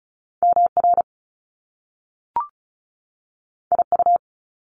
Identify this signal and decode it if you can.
Morse code, heard as a keyed tone, about 36 wpm, 700 Hz. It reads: MR IU
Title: Top Two Letter Combinations in Words 35wpm